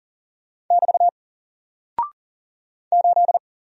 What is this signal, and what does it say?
Morse code, heard as a keyed tone, about 40 wpm, 700 Hz. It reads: <BT> 8
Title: Single Character (Letter:Number:Punctuation:Prosign) Alt 40wpm